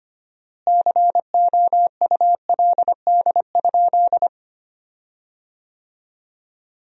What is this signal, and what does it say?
Morse code, heard as a keyed tone, about 25 wpm, 700 Hz. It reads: COULD?
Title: 4x Word-Spacing - Sentences from Top 400 Words - 25wpm